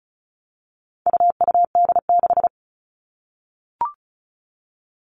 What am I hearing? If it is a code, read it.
Morse code, heard as a keyed tone, about 34 wpm, 700 Hz. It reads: UUD6
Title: All Cut Numbers - 4-Digit 35wpm